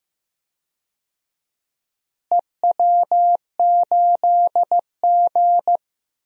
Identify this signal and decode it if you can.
Morse code, heard as a keyed tone, about 15 wpm, 700 Hz. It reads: EW8G
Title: DXCC-100 Call Signs - 4 Characters - Repeated 3x - 15wpm